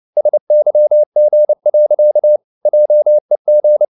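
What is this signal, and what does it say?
Morse code, heard as a keyed tone, about 29 wpm, 600 Hz. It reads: SYG. JEG
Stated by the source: Fast (QRQ) Morse at 29 wpm, from CW med Gnister.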